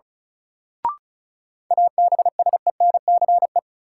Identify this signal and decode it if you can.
Morse code, heard as a keyed tone, about 34 wpm, 700 Hz. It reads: ABSENCE
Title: Sets of 4 Words - New Word Drill - 35wpm